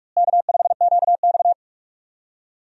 Morse code, heard as a keyed tone, about 42 wpm, 700 Hz. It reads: K5QX
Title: USA Call Signs - 4 Characters - Repeated 3x - 45wpm